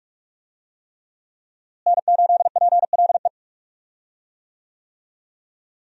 Morse code, heard as a keyed tone, about 46 wpm, 700 Hz. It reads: N8PLE
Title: USA Call Signs - 5 Characters - Repeated 3x - 45wpm